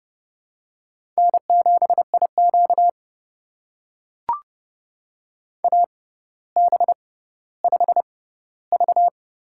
Morse code, heard as a keyed tone, about 30 wpm, spaced as for 15 wpm, 700 Hz. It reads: N7IQ AB5V
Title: USA Call Signs - 4 Characters - 6x Character Spacing - 30wpm